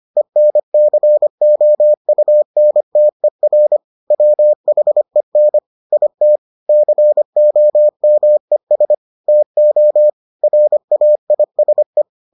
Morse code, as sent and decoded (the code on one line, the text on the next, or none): . -. -.-. --- ..- -. - . .-. / .-- .... -. / .. - / -.-. --- -- . ... / - --- / .-. .- .. ... .
ENCOUNTER WHEN IT COMES TO RAISE